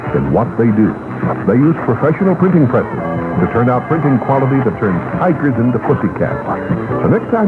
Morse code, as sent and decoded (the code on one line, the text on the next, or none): none